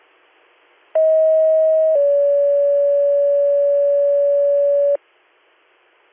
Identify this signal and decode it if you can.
Two-tone page, heard as a one-second tone, then a three-second tone, about 640 then 570 Hz. It is heard from 1.0 to 5.0 s.